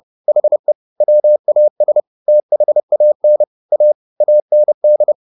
Morse code, heard as a keyed tone, about 30 wpm, 600 Hz. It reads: HE WAS THAN A AND
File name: WD104 30